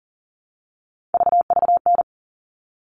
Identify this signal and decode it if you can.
Morse code, heard as a keyed tone, about 40 wpm, 700 Hz. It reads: VVN
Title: All Cut Numbers - 3-Digit 40wpm